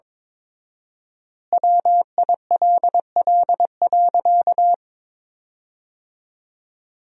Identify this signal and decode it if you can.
Morse code, heard as a keyed tone, about 21 wpm, 700 Hz. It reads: WILL.
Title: 4x Word-Spacing - Sentences from Top 200 Words - 22wpm